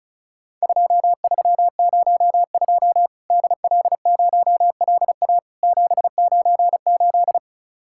Morse code, heard as a keyed tone, about 36 wpm, 700 Hz. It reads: 2302 DL0LA 798
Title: Worked All Europe DX Contest - QTC Exchange - 35wpm